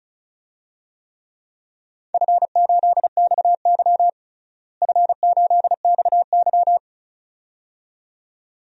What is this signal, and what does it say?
Morse code, heard as a keyed tone, about 36 wpm, 700 Hz. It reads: F8XY F8XY
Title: DXCC-100 Call Signs - 4 Characters - Repeated 2x - 35wpm